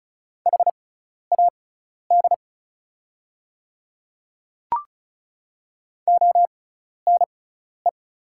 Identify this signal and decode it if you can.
Morse code, heard as a keyed tone, about 36 wpm, spaced as for 12 wpm, 690 Hz. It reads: HAD ONE
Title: 6x Character Spacing - Top 100 Words - 35wpm